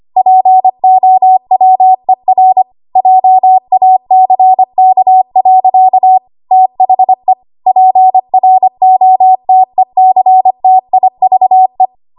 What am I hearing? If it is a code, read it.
Morse code, heard as a keyed tone, about 25 wpm, 750 Hz. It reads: POWER JACK. THE PROTECTIVE
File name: Archive / 25 WPM / 25WPM